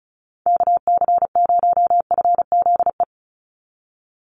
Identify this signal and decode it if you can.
Morse code, heard as a keyed tone, about 34 wpm, 700 Hz. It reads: KC0FZE
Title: USA Call Signs 35wpm